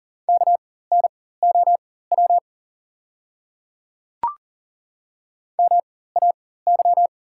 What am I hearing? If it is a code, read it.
Morse code, heard as a keyed tone, about 40 wpm, spaced as for 19 wpm, 690 Hz. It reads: KNOW MAY
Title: 4x Character Spacing - Top 100 Words - 40wpm